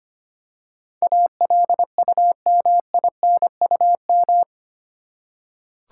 Morse code, heard as a keyed tone, about 25 wpm, 700 Hz. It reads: ALUMINUM
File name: Eight Letter Words - 25wpm